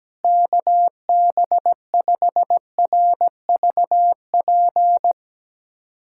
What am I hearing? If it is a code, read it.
Morse code, heard as a keyed tone, about 17 wpm, 700 Hz. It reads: KB5RVP
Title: USA Callsign DE Callsign 17wpm